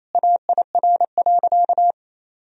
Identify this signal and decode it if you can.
Morse code, heard as a keyed tone, about 28 wpm, 700 Hz. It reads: AIR.
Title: Sentences from Top 1000 Words - Review Part 1 - 28wpm